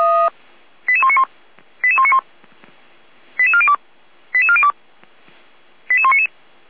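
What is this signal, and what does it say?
Five-tone selective call (ZVEI), heard as quick runs of five tones, tones of about 70 ms. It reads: 88191 88191 88402 88402 88199
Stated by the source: ZVEI five-tone call received by SDR, logged on the audio server of Feuerwehr Orlamünde (fire brigade).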